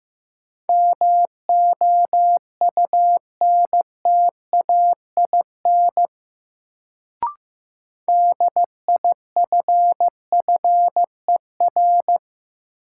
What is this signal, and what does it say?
Morse code, heard as a keyed tone, about 15 wpm, 700 Hz. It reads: MOUNTAIN DIFFER
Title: Sentences from Top 1000 Words - Review Part 1 - New Word Drill - 15wpm